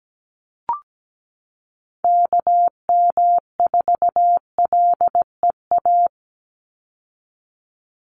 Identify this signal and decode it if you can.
Morse code, heard as a keyed tone, about 17 wpm, 700 Hz. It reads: KM4LEA
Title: USA Call Signs 17wpm